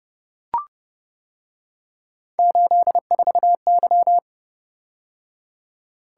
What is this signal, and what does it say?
Morse code, heard as a keyed tone, about 29 wpm, 700 Hz. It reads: 84Y